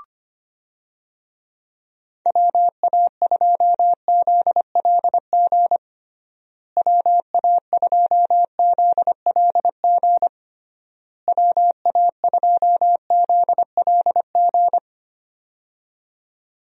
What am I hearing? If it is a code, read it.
Morse code, heard as a keyed tone, about 25 wpm, 700 Hz. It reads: WA2ZLG WA2ZLG WA2ZLG
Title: USA Call Signs - 6 Characters - Repeated 3x - 25wpm